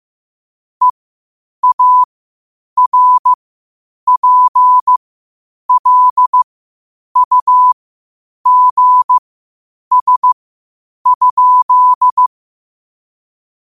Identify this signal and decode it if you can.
Morse code, heard as a keyed tone, about 15 wpm, spaced as for 8 wpm, 1000 Hz. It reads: EARPLUGS?
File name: qotdfiles / 8wpm